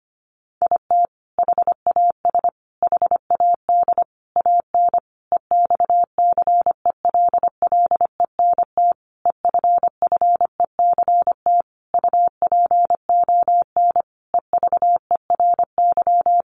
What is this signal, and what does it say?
Morse code, heard as a keyed tone, about 25 wpm, 700 Hz. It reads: IT HAS HAD AN EXCELLENT EFFECT UPON EVERY
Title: Sentences from Top 1100 Words 25wpm